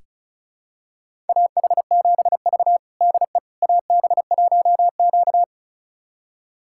Morse code, heard as a keyed tone, about 36 wpm, 700 Hz. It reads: AH7V DE AB1Q
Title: USA Callsign DE Callsign - 4 Characters - 35wpm